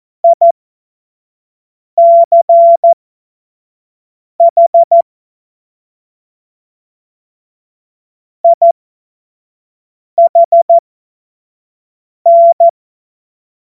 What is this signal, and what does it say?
Morse code, heard as a keyed tone, about 14 wpm, spaced as for 5 wpm, 680 Hz. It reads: ICH IHN
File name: Morsekurs / Witze / Witz 17. Mai 2022 25 F70